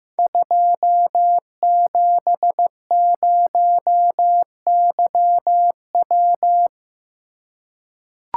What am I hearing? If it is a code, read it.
Morse code, heard as a keyed tone, about 15 wpm, 700 Hz. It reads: 270YW
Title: Code Groups - 5 Letter:Numbers - 15wpm